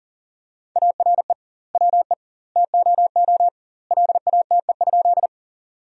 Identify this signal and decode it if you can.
Morse code, heard as a keyed tone, about 37 wpm, 700 Hz. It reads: ARE WE TOO LATE?
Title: 2x Word-Spacing - Sentences from Top 300 Words - 40wpm